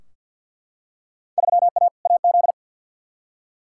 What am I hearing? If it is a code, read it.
Morse code, heard as a keyed tone, about 51 wpm, 700 Hz. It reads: 3A AZ
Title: Field Day Exchanges - 50wpm